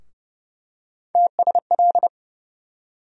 Morse code, heard as a keyed tone, about 32 wpm, 700 Hz. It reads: TSL
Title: USA Call Sign Postfixes - 3 Letters 30wpm